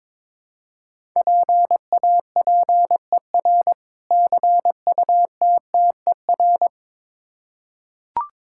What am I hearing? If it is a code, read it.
Morse code, heard as a keyed tone, about 22 wpm, 700 Hz. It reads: PAPER CUTTER
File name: Difficult Words with Hints - 22wpm